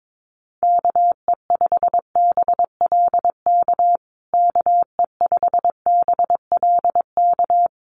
Morse code, heard as a keyed tone, about 22 wpm, 700 Hz. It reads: KE5BLK KE5BLK
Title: USA Call Signs Repeated 2x 22wpm